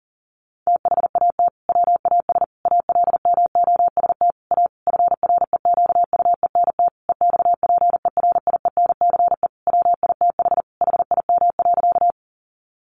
Morse code, heard as a keyed tone, about 40 wpm, 700 Hz. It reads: THAT WAS ALMOST A FREQUENT EXPERIENCE WITH HIM.
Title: Sentences from Top 1400 Words 40wpm